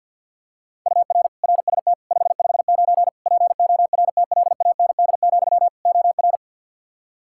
Morse code, heard as a keyed tone, about 48 wpm, 700 Hz. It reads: UR RST 559 PORTLAND, OR
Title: K1USN SST QSO Practice 50wpm